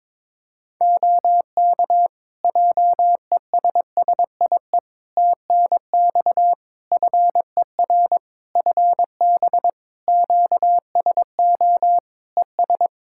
Morse code, heard as a keyed tone, about 23 wpm, 700 Hz. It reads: OK JESSIE TNX FER FB QSO ES
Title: Practice QSO - Exchange 4 - Ending - 22wpm